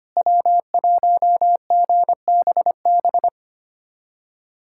Morse code, heard as a keyed tone, about 25 wpm, 700 Hz. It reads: W1GBB
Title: USA Callsign DE Callsign 25wpm